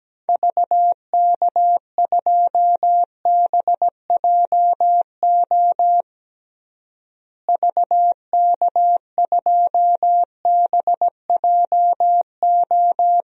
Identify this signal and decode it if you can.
Morse code, heard as a keyed tone, about 17 wpm, 700 Hz. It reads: VK2BJO VK2BJO
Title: DXCC-100 Call Signs - 6 Characters - Repeated 2x - 17wpm